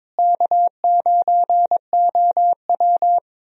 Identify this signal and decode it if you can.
Morse code, heard as a keyed tone, about 22 wpm, 700 Hz. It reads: K9OW
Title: USA Call Signs - 4 Characters - Repeated 3x - 22wpm